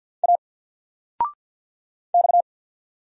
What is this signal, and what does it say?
Morse code, heard as a keyed tone, about 49 wpm, 700 Hz. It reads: A X
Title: Single Letter-Number Alt 50wpm